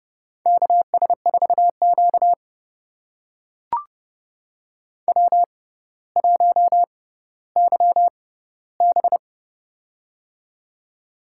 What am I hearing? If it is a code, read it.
Morse code, heard as a keyed tone, about 31 wpm, spaced as for 16 wpm, 700 Hz. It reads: KS4Q W1YB
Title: USA Call Signs - 4 Characters - 6x Character Spacing - 30wpm